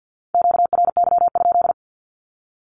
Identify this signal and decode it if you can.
Morse code, heard as a keyed tone, about 50 wpm, 700 Hz. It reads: QRY?
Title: Q-Codes with Spoken Meaning - 50wpm